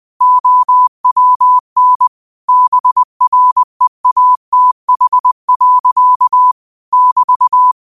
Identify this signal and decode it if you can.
Morse code, heard as a keyed tone, about 20 wpm, 1000 Hz. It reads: OWN BREATH. <BT>